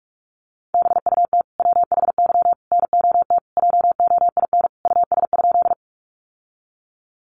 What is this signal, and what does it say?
Morse code, heard as a keyed tone, about 44 wpm, 700 Hz. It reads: BUT WHY NOT JOIN US?
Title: Sentences from Top 1400 Words 45wpm